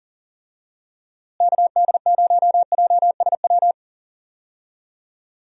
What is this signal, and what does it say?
Morse code, heard as a keyed tone, about 38 wpm, 700 Hz. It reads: KD0JSW
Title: USA Call Signs - 6 Characters - Repeated 3x - 40wpm